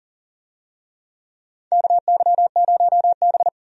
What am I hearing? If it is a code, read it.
Morse code, heard as a keyed tone, about 40 wpm, 700 Hz. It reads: KY0B
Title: USA Call Signs - 4 Characters - Repeated 3x - 40wpm